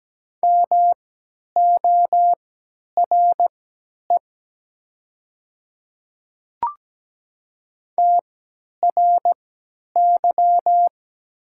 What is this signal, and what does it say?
Morse code, heard as a keyed tone, about 17 wpm, spaced as for 10 wpm, 700 Hz. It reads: MORE TRY
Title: 3x Character Spacing - Top 200 Words - 17wpm